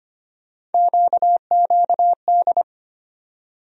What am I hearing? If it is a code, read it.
Morse code, heard as a keyed tone, about 25 wpm, 700 Hz. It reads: QQD
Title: USA Call Sign Postfixes - 3 Letters 25wpm